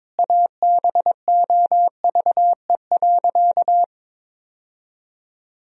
Morse code, heard as a keyed tone, about 22 wpm, 700 Hz. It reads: ABOVE.
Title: Sentences from Top 1000 Words - Review Part 2 - 22wpm